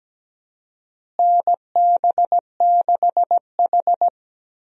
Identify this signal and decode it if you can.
Morse code, heard as a keyed tone, about 17 wpm, 700 Hz. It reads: NB6H
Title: USA Call Signs - 4 Characters - Repeated 3x - 17wpm